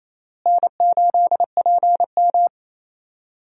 Morse code, heard as a keyed tone, about 28 wpm, 700 Hz. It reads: N8PM
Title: USA Call Signs - 4 Characters - Repeated 2x - 28wpm